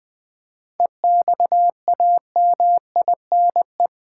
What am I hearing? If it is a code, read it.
Morse code, heard as a keyed tone, about 19 wpm, 700 Hz. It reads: EXAMINE
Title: Seven Letter Words - 20wpm